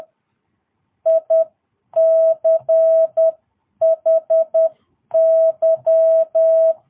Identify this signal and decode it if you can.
Morse code, heard as a keyed tone, about 10 wpm, spaced as for 9 wpm, 650 Hz. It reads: ICHY
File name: jh8cbh / 2022cw